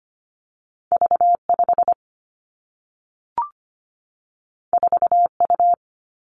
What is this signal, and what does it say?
Morse code, heard as a keyed tone, about 25 wpm, 700 Hz. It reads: V 5 4 U
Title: Two Challenging Letter-Number Combinations 25wpm